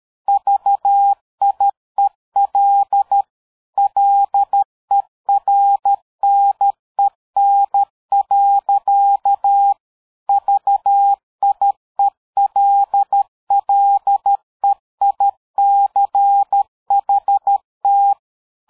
Morse code, heard as a keyed tone, about 13 wpm, 800 Hz. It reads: VIEL LERNEN. VIELLEICHT
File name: dk5ke / PersWorte 60